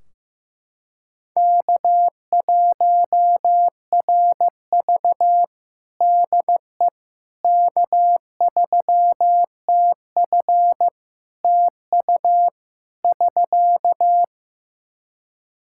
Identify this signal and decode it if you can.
Morse code, heard as a keyed tone, about 15 wpm, 700 Hz. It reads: K1RV DE K3TF TU <SK>
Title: Practice QSO - Exchange 4 - Ending - 15wpm